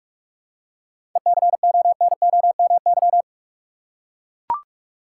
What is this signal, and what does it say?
Morse code, heard as a keyed tone, about 44 wpm, 700 Hz. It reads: ECONOMY
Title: Seven Letter Words - 45wpm